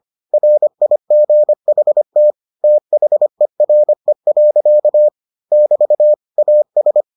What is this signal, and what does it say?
Morse code, heard as a keyed tone, about 25 wpm, 600 Hz. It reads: RIGHT THERE. <BT> AS